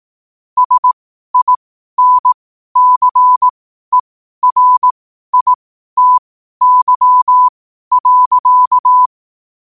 Morse code, heard as a keyed tone, about 18 wpm, spaced as for 12 wpm, 1000 Hz. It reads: SINCERITY.
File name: qotdfiles / 12wpm